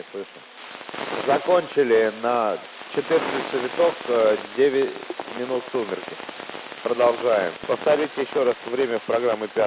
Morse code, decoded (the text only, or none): none